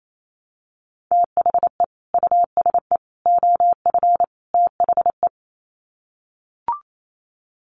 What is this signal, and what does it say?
Morse code, heard as a keyed tone, about 28 wpm, 700 Hz. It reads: THE USE OF THE
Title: Sets of 4 Words 28wpm